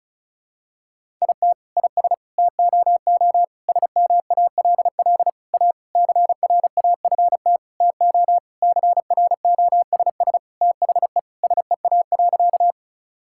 Morse code, heard as a keyed tone, about 36 wpm, 700 Hz. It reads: IT IS TOO SMALL A CRAFT TO CROSS THE SEA.
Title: Sentences from Top 1000 Words - Review Part 6 - 35wpm